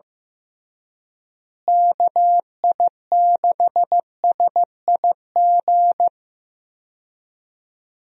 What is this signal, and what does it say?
Morse code, heard as a keyed tone, about 15 wpm, 700 Hz. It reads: KI6SIG
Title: USA Call Signs - 6 Characters - Repeated 3x - 15wpm